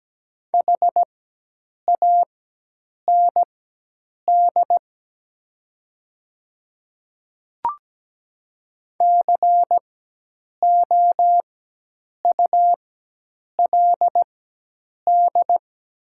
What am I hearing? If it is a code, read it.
Morse code, heard as a keyed tone, about 17 wpm, spaced as for 8 wpm, 700 Hz. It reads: HAND COULD
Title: 4x Character Spacing - Top 200 Words - 17wpm